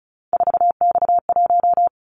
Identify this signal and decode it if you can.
Morse code, heard as a keyed tone, about 36 wpm, 690 Hz. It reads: 4X1
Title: Callsign Prefixes for Top 50 DXCC - 35wpm